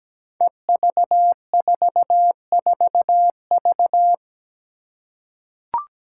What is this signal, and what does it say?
Morse code, heard as a keyed tone, about 17 wpm, 700 Hz. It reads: EV44V
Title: Five Challenging Letter-Number Combinations 17wpm